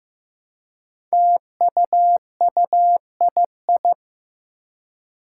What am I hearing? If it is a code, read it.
Morse code, heard as a keyed tone, about 15 wpm, 700 Hz. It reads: TUUII
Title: Five Challenging Letter-Number Combinations 15wpm